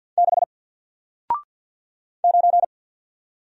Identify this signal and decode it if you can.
Morse code, heard as a keyed tone, about 50 wpm, 700 Hz. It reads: 6 9
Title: Single Character (Letter:Number:Punctuation:Prosign) Alt 50wpm